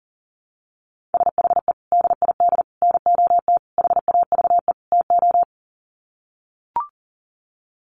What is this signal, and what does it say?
Morse code, heard as a keyed tone, about 40 wpm, 700 Hz. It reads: SHE DID NOT HAVE TO
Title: Sets of 5 Words Encore2 40wpm